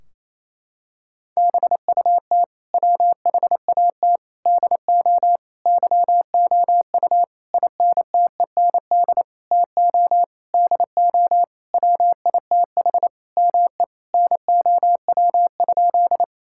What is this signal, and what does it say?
Morse code, heard as a keyed tone, about 28 wpm, 700 Hz. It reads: BUT WHAT DO YOU INTEND TO DO WITH ME NOW?
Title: Sentences from Top 1600 Words 28wpm